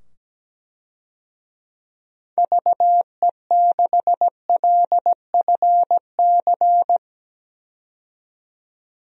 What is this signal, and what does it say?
Morse code, heard as a keyed tone, about 17 wpm, 700 Hz. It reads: VE6LFC
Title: DXCC-100 Call Signs - 6 Characters - 17wpm